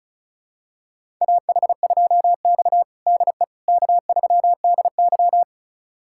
Morse code, heard as a keyed tone, about 36 wpm, 700 Hz. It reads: AH2X DE K3DY
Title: USA Callsign DE Callsign - 4 Characters - 35wpm